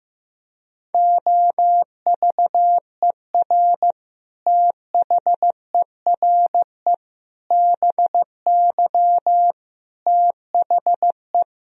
Morse code, heard as a keyed tone, about 15 wpm, 700 Hz. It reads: OVER THERE BY THE
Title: Sentences from Top 1000 Words - Review Part 3 - 15wpm